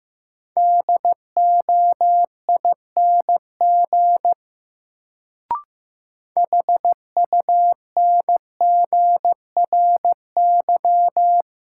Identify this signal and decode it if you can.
Morse code, heard as a keyed tone, about 15 wpm, 700 Hz. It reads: DOING HUNGRY
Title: Sentences from Top 1400 Words - New Word Drill - 15wpm